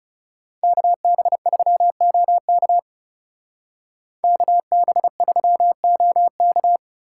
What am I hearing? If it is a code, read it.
Morse code, heard as a keyed tone, about 34 wpm, 700 Hz. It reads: KB3OK KB3OK
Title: Speed Racing - USA 2x2 Call Signs 30wpm